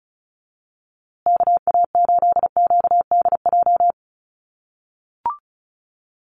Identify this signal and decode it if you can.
Morse code, heard as a keyed tone, about 34 wpm, 700 Hz. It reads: KA8QDJ
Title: USA Call Signs Repeated 3x 35wpm